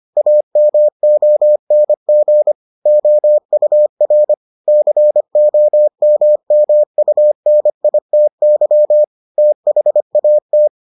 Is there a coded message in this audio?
Yes: Morse code from 0.2 to 10.7 s, heard as a keyed tone, about 25 wpm, 600 Hz. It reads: AMONG OUR COMMUNITY THAT